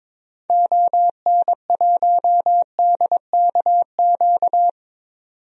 Morse code, heard as a keyed tone, about 22 wpm, 700 Hz. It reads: ON1DKQ
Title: DXCC-100 Call Signs - 6 Characters - 22wpm